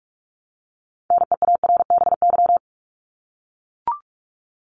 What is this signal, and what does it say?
Morse code, heard as a keyed tone, about 44 wpm, 700 Hz. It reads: NEARBY
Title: Top 1501-1750 Words 45wpm